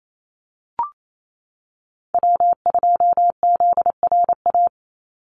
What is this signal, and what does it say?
Morse code, heard as a keyed tone, about 28 wpm, 700 Hz. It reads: W2ZRA
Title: Call Signs - 28wpm